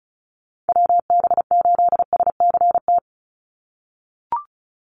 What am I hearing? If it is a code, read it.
Morse code, heard as a keyed tone, about 36 wpm, 700 Hz. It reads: WB8SCT
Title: Call Signs - 35wpm